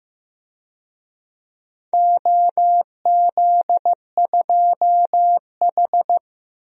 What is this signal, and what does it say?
Morse code, heard as a keyed tone, about 15 wpm, 700 Hz. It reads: OZ2H
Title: DXCC-100 Call Signs - 4 Characters - Repeated 3x - 15wpm